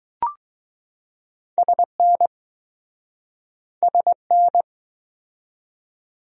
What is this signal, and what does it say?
Morse code, heard as a keyed tone, about 20 wpm, 700 Hz. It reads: SN SN